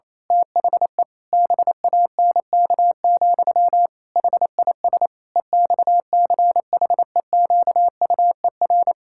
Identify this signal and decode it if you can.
Morse code, heard as a keyed tone, about 28 wpm, 700 Hz. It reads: THE BANK, HIS EXCHEQUER
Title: Pangram Challenge v2 - 28wpm